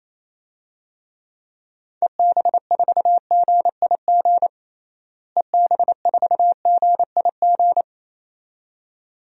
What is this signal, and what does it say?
Morse code, heard as a keyed tone, about 28 wpm, 700 Hz. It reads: EB4GIG EB4GIG
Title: DXCC-100 Call Signs - 6 Characters - Repeated 2x - 28wpm